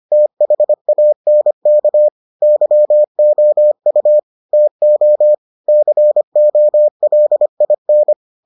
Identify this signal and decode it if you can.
Morse code, heard as a keyed tone, about 25 wpm, 600 Hz. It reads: THANK YOU TO COLIN